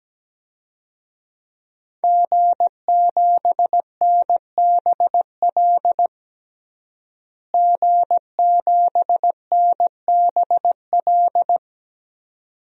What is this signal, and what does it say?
Morse code, heard as a keyed tone, about 17 wpm, 700 Hz. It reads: G7NBL G7NBL
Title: DXCC-100 Call Signs - 5 Characters - Repeated 2x - 17wpm